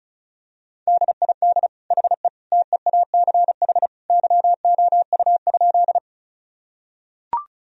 Morse code, heard as a keyed tone, about 34 wpm, 700 Hz. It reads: DID HE TEACH YOU?
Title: Sentences from Top 1000 Words - Review Part 2 - 35wpm